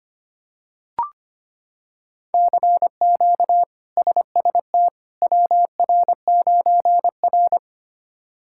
Morse code, heard as a keyed tone, about 25 wpm, 700 Hz. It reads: CQ SST WR9R
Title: K1USN SST QSO Practice - 25wpm